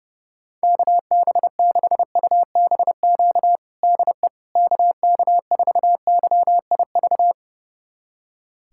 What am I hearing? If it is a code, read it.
Morse code, heard as a keyed tone, about 30 wpm, 700 Hz. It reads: KB6UBQ DE KK4YIV
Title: USA Callsign DE Callsign - 6 Characters - 30wpm